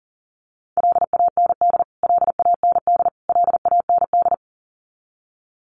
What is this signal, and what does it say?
Morse code, heard as a keyed tone, about 40 wpm, 700 Hz. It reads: LAND LAND LAND
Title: Top 300 Words Repeated 40wpm